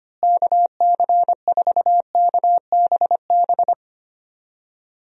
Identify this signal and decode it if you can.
Morse code, heard as a keyed tone, about 25 wpm, 700 Hz. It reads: KC4KBB
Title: USA Call Signs - 6 Characters - Repeated 3x - 25wpm